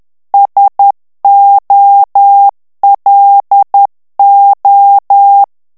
Morse code, heard as a keyed tone, about 10 wpm, 800 Hz. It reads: SOLO
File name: ITALIEN texte clair 2 vitesse 10 mots minute